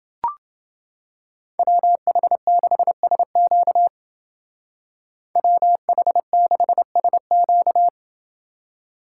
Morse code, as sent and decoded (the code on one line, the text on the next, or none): .-- .... -.... ... --.- / .-- .... -.... ... --.-
WH6SQ WH6SQ